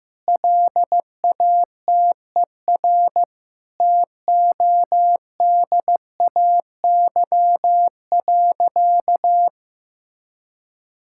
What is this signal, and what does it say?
Morse code, heard as a keyed tone, about 16 wpm, 700 Hz. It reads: LATER TODAY.
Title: Sentences from Top 1000 Words - Review Part 5 - 15wpm